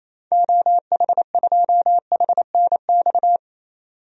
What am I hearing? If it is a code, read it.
Morse code, heard as a keyed tone, about 28 wpm, 700 Hz. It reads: OH2HNX
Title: DXCC-100 Call Signs - 6 Characters - 28wpm